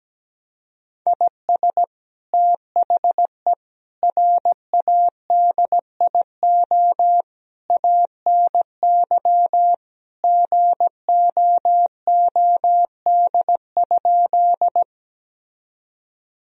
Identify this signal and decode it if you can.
Morse code, heard as a keyed tone, about 17 wpm, 700 Hz. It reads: IS THE RADIO ANY GOOD?
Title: Sentences from Top 1000 Words - Review Part 3 - 17wpm